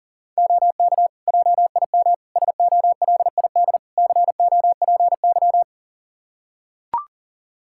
Morse code, heard as a keyed tone, about 40 wpm, 700 Hz. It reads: OK JIM SOLID COPY
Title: Practice QSO - Exchange 3 - More Info - 40wpm